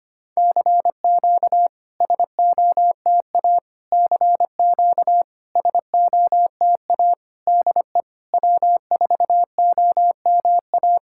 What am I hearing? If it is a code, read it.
Morse code, heard as a keyed tone, about 25 wpm, 690 Hz. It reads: CQ SOTA CQ SOTA DE W4OMA
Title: SOTA QSO Practice 25wpm